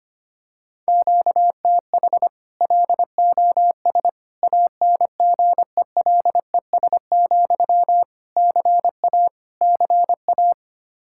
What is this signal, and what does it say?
Morse code, heard as a keyed tone, about 25 wpm, 700 Hz. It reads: QTH LOS ANGELES, CA CA
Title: Practice QSO - Exchange 1 - Beginning - 25wpm